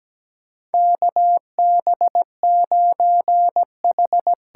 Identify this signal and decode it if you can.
Morse code, heard as a keyed tone, about 17 wpm, 700 Hz. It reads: KB9H
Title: USA Call Signs - 6 Characters - 8x Character Spacing - 17wpm